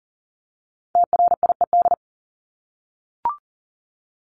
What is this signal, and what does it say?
Morse code, heard as a keyed tone, about 40 wpm, 700 Hz. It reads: TRIED
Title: ED Suffix - Top 100 Words with Base Words 40wpm